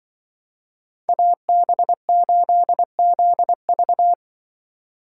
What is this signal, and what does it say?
Morse code, heard as a keyed tone, about 23 wpm, 700 Hz. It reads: AB8ZV
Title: Speed Racing - USA 2x2 Call Signs 20wpm